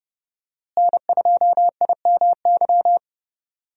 Morse code, heard as a keyed tone, about 31 wpm, 700 Hz. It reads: N2IMY
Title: USA Call Signs - 5 Characters - Repeated 2x - 30wpm